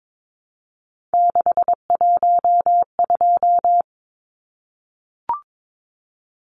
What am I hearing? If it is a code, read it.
Morse code, heard as a keyed tone, about 22 wpm, 700 Hz. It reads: 612
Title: Numbers - 3-Digit 22wpm